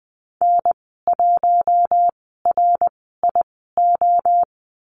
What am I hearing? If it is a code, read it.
Morse code, heard as a keyed tone, about 20 wpm, spaced as for 14 wpm, 700 Hz. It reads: N1RIO
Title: 069 Lesson 12g - Letter d - Call Signs 20wpm